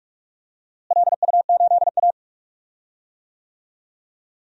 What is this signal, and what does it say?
Morse code, heard as a keyed tone, about 43 wpm, 700 Hz. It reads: LU8A
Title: DXCC-100 Call Signs - 4 Characters - Repeated 3x - 45wpm